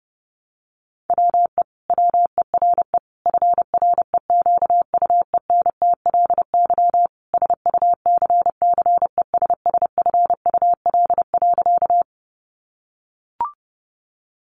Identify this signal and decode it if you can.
Morse code, heard as a keyed tone, about 30 wpm, 700 Hz. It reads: WE WERE FREQUENTLY SUCCESSFUL.
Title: Sentences from Top 1300 Words 30wpm